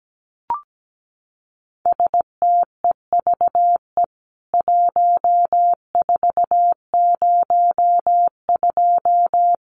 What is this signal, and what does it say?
Morse code, heard as a keyed tone, about 17 wpm, 700 Hz. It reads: STEVE 1402
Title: CWT Exchange 17wpm